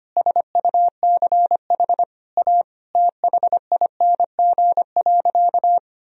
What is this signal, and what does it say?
Morse code, heard as a keyed tone, about 25 wpm, 700 Hz. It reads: SUCH A THING.
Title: Sentences from Top 1000 Words - Review Part 1 - 25wpm